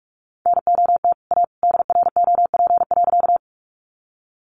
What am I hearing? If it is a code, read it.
Morse code, heard as a keyed tone, about 44 wpm, 700 Hz. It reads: NOT A DROP.
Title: Sentences from Top 1400 Words 45wpm